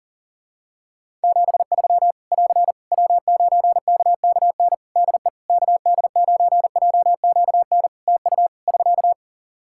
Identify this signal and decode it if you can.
Morse code, heard as a keyed tone, about 40 wpm, 700 Hz. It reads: 73 <AR> W9KKN DE KD9JQN TU <SK>
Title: Practice QSO - Exchange 4 - Ending - 40wpm